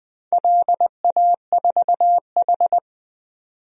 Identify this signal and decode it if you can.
Morse code, heard as a keyed tone, about 20 wpm, 700 Hz. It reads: LA4H
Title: DXCC-100 Call Signs - 4 Characters - 20wpm